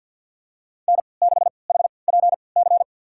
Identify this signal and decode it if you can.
Morse code, heard as a keyed tone, about 47 wpm, spaced as for 28 wpm, 690 Hz. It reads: N6HPC
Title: USA Call Signs - 5 Characters - 3x Character Spacing - 50wpm